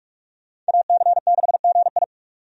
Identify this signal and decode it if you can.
Morse code, heard as a keyed tone, about 46 wpm, 700 Hz. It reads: AC6GI
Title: USA Call Signs - 5 Characters - 45wpm